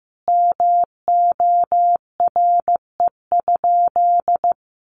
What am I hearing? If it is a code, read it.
Morse code, heard as a keyed tone, about 15 wpm, 700 Hz. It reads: MORE?
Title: Sentences from Top 1300 Words 15wpm